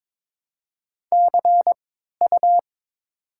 Morse code, heard as a keyed tone, about 21 wpm, 700 Hz. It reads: C U
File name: Top 90 QSO Elements - 3x Character Spacing - 22wpm